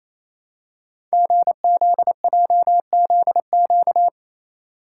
Morse code, heard as a keyed tone, about 29 wpm, 700 Hz. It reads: GZJZQ
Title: Code Groups - 5 Letter:Numbers - 28wpm